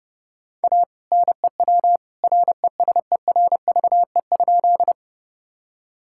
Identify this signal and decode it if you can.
Morse code, heard as a keyed tone, about 30 wpm, 700 Hz. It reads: A NEW RESERVE?
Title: Sentences from Top 1000 Words - Review Part 6 - 30wpm